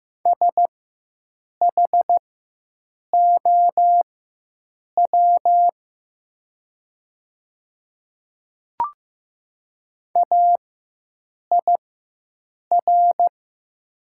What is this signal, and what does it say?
Morse code, heard as a keyed tone, about 15 wpm, spaced as for 7 wpm, 700 Hz. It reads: SHOW AIR
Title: 4x Character Spacing - Top 200 Words - 15wpm